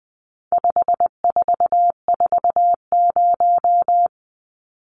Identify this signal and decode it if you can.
Morse code, heard as a keyed tone, about 20 wpm, 700 Hz. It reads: 5440
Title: Numbers - 4-Digit 20wpm